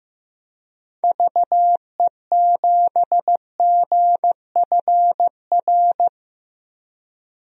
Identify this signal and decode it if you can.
Morse code, heard as a keyed tone, about 16 wpm, 700 Hz. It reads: VE7GFR
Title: DXCC-100 Call Signs - 6 Characters - Repeated 3x - 15wpm